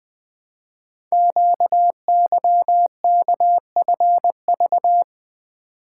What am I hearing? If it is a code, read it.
Morse code, heard as a keyed tone, about 20 wpm, 700 Hz. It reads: QYKFV
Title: Code Groups - 5 Letters - 20wpm